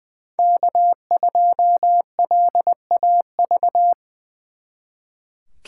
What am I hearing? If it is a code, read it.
Morse code, heard as a keyed tone, about 23 wpm, 700 Hz. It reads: K2LAV
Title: Speed Racing - USA 1x3 Call Signs 20wpm